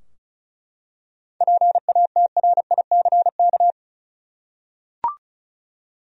Morse code, heard as a keyed tone, about 34 wpm, 700 Hz. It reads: PATRICK
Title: USA Common Names - 35wpm